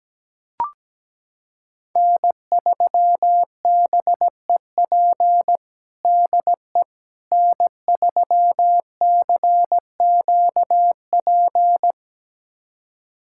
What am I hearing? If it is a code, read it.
Morse code, heard as a keyed tone, about 17 wpm, 700 Hz. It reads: N3BEP DE N3CQP
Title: USA Callsign DE Callsign - 5 Characters - 17wpm